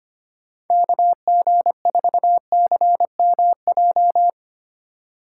Morse code, heard as a keyed tone, about 25 wpm, 700 Hz. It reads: KG4CMJ
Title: USA Call Signs - 4x Character Spacing - 25wpm